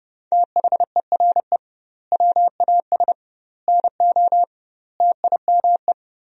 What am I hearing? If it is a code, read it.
Morse code, heard as a keyed tone, about 30 wpm, 700 Hz. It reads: THERE WAS NO TIME
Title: 2x Word-Spacing - Sentences from Top 400 Words - 30wpm